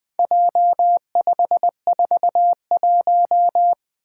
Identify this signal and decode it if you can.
Morse code, heard as a keyed tone, about 20 wpm, 700 Hz. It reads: J541
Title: Four Letter-Number Combinations - 20wpm